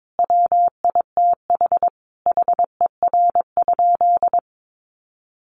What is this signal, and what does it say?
Morse code, heard as a keyed tone, about 22 wpm, 700 Hz. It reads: WITH HER?
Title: Sentences from Top 700 Words 22wpm